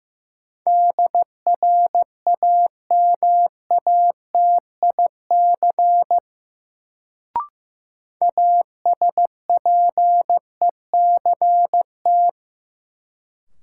Morse code, heard as a keyed tone, about 15 wpm, 700 Hz. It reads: DRAMATIC ASPECT
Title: Sentences from Top 1300 Words - New Word Drill Alt - 15wpm